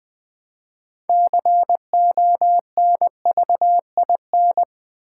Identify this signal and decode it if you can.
Morse code, heard as a keyed tone, about 20 wpm, 700 Hz. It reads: CONVIN
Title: Build A Word - 3 Letters At A Time - 9-Letter Words - 20wpm